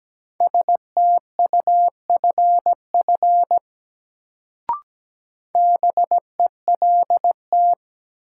Morse code, heard as a keyed tone, about 17 wpm, 700 Hz. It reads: STUFF BELT
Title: Sentences from Top 1400 Words - New Word Drill - 17wpm